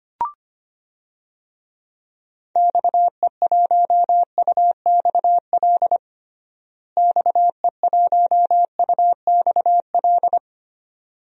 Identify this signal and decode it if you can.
Morse code, heard as a keyed tone, about 25 wpm, 700 Hz. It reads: XE1UXL XE1UXL
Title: DXCC-100 Call Signs - 6 Characters - Repeated 2x - 25wpm